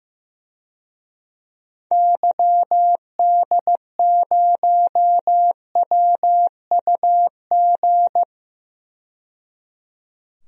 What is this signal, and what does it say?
Morse code, heard as a keyed tone, about 15 wpm, 700 Hz. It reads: YD0WUG
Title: DXCC-100 Call Signs - 6 Characters - 15wpm